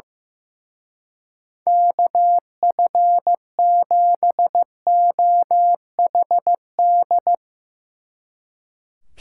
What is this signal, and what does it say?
Morse code, heard as a keyed tone, about 16 wpm, 700 Hz. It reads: KF7OHD
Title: Speed Racing - USA 2x3 Call Signs 15wpm